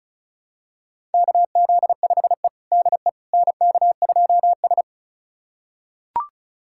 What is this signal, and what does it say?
Morse code, heard as a keyed tone, about 36 wpm, 700 Hz. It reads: KZ5E DE NK2S
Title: USA Callsign DE Callsign - 4 Characters - 35wpm